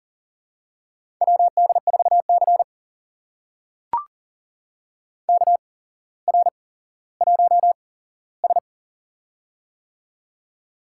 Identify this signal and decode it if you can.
Morse code, heard as a keyed tone, about 42 wpm, spaced as for 17 wpm, 700 Hz. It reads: WD4C KR1S
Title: USA Call Signs - 4 Characters - 8x Character Spacing - 40wpm